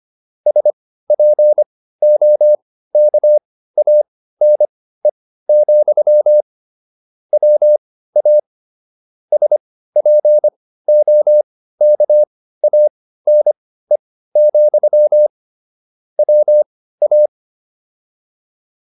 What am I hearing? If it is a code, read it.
Morse code, heard as a keyed tone, about 25 wpm, spaced as for 15 wpm, 600 Hz. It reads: SPOKANE, WA SPOKANE, WA